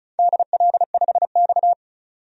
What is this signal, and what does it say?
Morse code, heard as a keyed tone, about 35 wpm, 700 Hz. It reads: DL5X
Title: DXCC-100 Call Signs - 4 Characters - 35wpm